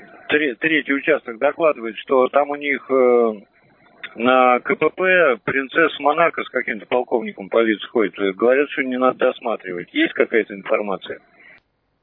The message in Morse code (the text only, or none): none